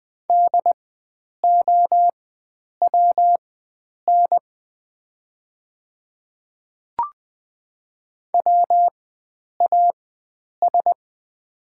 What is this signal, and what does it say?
Morse code, heard as a keyed tone, about 20 wpm, spaced as for 9 wpm, 700 Hz. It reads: DOWN WAS
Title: 4x Character Spacing - Top 500 Words - 20wpm